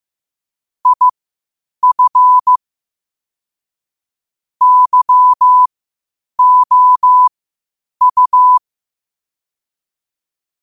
Morse code, heard as a keyed tone, about 15 wpm, spaced as for 8 wpm, 1000 Hz. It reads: IF YOU
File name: qotdfiles / 8wpm